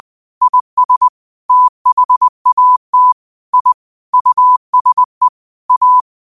Morse code, heard as a keyed tone, about 20 wpm, 1000 Hz. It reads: IS THAT I USE A
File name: qotdfiles / 20wpm